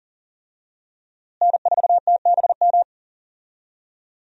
Morse code, heard as a keyed tone, about 40 wpm, 700 Hz. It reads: N4TBM
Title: USA Call Signs - 5 Characters - Repeated 3x - 40wpm